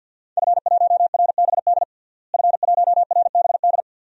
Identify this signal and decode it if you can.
Morse code, heard as a keyed tone, about 46 wpm, 700 Hz. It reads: F1RBD F1RBD
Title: DXCC-100 Call Signs - 5 Characters - Repeated 2x - 50wpm